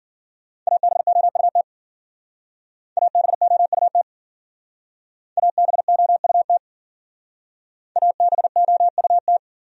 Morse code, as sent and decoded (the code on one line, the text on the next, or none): .- -... --- ..- - / .- -... --- ..- - / .- -... --- ..- - / .- -... --- ..- -
ABOUT ABOUT ABOUT ABOUT